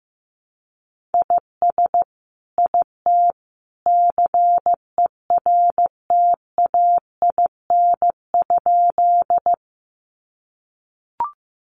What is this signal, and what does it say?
Morse code, heard as a keyed tone, about 15 wpm, 700 Hz. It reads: IS IT CERTAIN?
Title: Sentences from Top 500 Words 15wpm